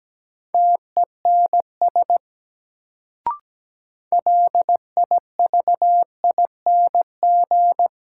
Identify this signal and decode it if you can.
Morse code, heard as a keyed tone, about 17 wpm, 700 Hz. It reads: TENS LIVING
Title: Sets of 4 Words - New Word Drill - 17wpm